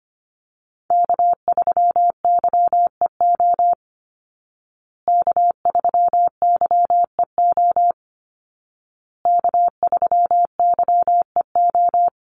Morse code, heard as a keyed tone, about 25 wpm, 700 Hz. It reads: K3YEO K3YEO K3YEO
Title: Call Signs Repeated 2x - 25wpm